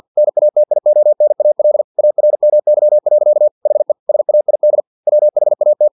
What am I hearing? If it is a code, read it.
Morse code, heard as a keyed tone, about 50 wpm, 600 Hz. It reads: NATIONAL ARMY. HE SAID WHAT
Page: Never 49wpm